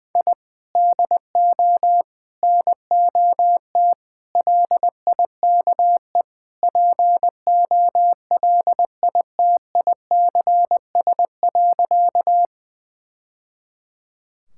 Morse code, heard as a keyed tone, about 20 wpm, 700 Hz. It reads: I DO NOT LIKE POLITICS.
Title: Sentences from Top 1000 Words - Review Part 6 - 20wpm